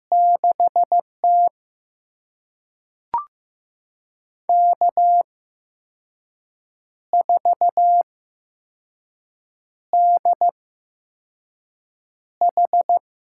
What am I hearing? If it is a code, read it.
Morse code, heard as a keyed tone, about 15 wpm, spaced as for 5 wpm, 700 Hz. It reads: 6T K4DH
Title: USA Call Signs - 4 Characters - 8x Character Spacing - 15wpm